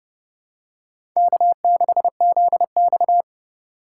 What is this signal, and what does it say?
Morse code, heard as a keyed tone, about 30 wpm, 700 Hz. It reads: K6ZX
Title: K1USN SST QSO Practice - 30wpm